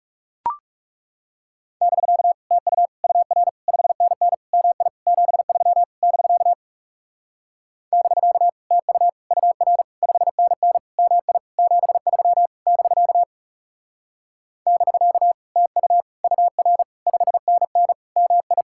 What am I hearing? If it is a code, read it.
Morse code, heard as a keyed tone, about 38 wpm, 700 Hz. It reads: <BK> TU UR 5NN MI 73 <BK> <BK> TU UR 5NN MI 73 <BK> <BK> TU UR 5NN MI